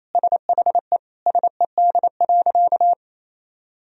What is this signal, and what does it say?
Morse code, heard as a keyed tone, about 28 wpm, 700 Hz. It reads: SHE SED.
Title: Sentences from Top 100 Words and Abbreviations 28wpm